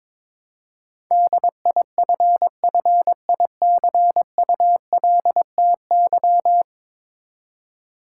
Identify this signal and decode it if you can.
Morse code, heard as a keyed tone, about 22 wpm, 700 Hz. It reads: DIFFICULTY
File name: practice / 22wpm / Ten Letter Words - 22wpm